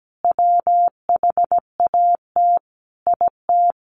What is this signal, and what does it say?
Morse code, heard as a keyed tone, about 17 wpm, 700 Hz. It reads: WHAT IT
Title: Sentences from Top 700 Words 17wpm